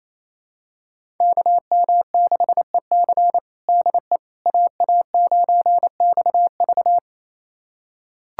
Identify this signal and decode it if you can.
Morse code, heard as a keyed tone, about 28 wpm, 700 Hz. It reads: KM6EC DE AA9XV
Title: USA Callsign DE Callsign - 5 Characters - 28wpm